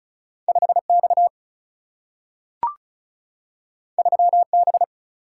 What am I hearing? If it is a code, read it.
Morse code, heard as a keyed tone, about 36 wpm, 700 Hz. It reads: 5X 3B
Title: Single-Letter Single-Number 35wpm